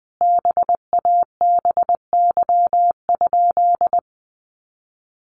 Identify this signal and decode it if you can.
Morse code, heard as a keyed tone, about 20 wpm, 700 Hz. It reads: BABY?
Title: Sentences from Top 1300 Words 20wpm